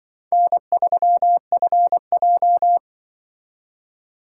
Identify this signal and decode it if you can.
Morse code, heard as a keyed tone, about 25 wpm, 700 Hz. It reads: N3FJ
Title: Speed Racing - USA 1x2 Call Signs 20wpm